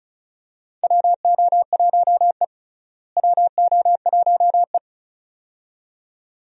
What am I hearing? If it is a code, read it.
Morse code, heard as a keyed tone, about 34 wpm, 700 Hz. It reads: WO1E WO1E
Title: USA Call Signs - 4 Characters - Repeated 2x - 35wpm